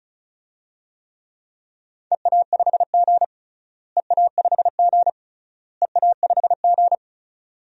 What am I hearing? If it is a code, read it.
Morse code, heard as a keyed tone, about 36 wpm, 700 Hz. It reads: EA5G EA5G EA5G
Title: DXCC-100 Call Signs - 4 Characters - Repeated 3x - 35wpm